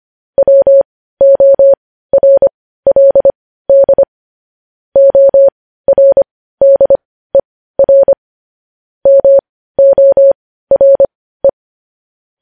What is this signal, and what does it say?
Morse code, heard as a keyed tone, about 25 wpm, spaced as for 15 wpm, 550 Hz. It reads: WORLD ORDER MORE
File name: podcast / News headlines-1x-25x15